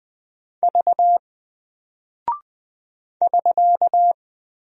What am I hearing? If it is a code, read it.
Morse code, heard as a keyed tone, about 20 wpm, 700 Hz. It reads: V <SK>
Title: Single Character (Letter:Number:Punctuation:Prosign) Alt 20wpm